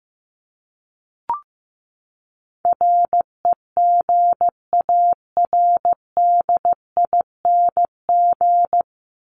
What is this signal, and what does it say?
Morse code, heard as a keyed tone, about 15 wpm, 700 Hz. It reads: REGARDING
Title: ING Suffix - Top 100 Words with Base Words 15wpm